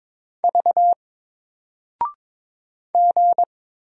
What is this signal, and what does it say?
Morse code, heard as a keyed tone, about 22 wpm, 700 Hz. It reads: V G
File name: Single-Letter-Number-22wpm